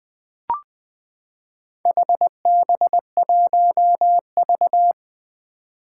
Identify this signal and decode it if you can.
Morse code, heard as a keyed tone, about 20 wpm, 700 Hz. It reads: HB1V
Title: Four Challenging Letter-Number Combinations 20wpm